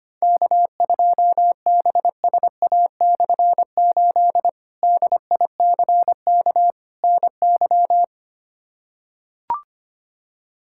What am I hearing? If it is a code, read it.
Morse code, heard as a keyed tone, about 25 wpm, 700 Hz. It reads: K2BSA/8 DICK NY
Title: K1USN SST QSO Practice - 25wpm